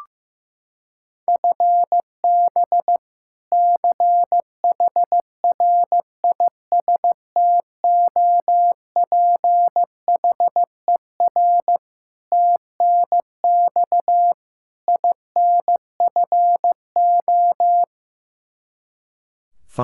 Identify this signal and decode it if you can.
Morse code, heard as a keyed tone, about 15 wpm, 700 Hz. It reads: FB CHRISTOPHER TNX INFO
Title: K1USN SST QSO Practice 15wpm